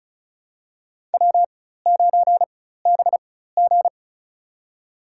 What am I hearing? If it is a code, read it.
Morse code, heard as a keyed tone, about 36 wpm, spaced as for 16 wpm, 700 Hz. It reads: W9BG
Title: USA Call Signs - 4 Characters - 4x Character Spacing - 35wpm